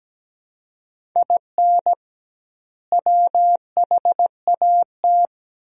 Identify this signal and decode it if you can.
Morse code, heard as a keyed tone, about 17 wpm, 700 Hz. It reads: IN WHAT
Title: 2x Word-Spacing - Sentences from Top 200 Words - 17wpm